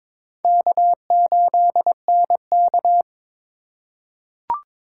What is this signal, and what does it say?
Morse code, heard as a keyed tone, about 21 wpm, 700 Hz. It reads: K8NK
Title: USA Call Signs - 4 Characters - 3x Character Spacing - 22wpm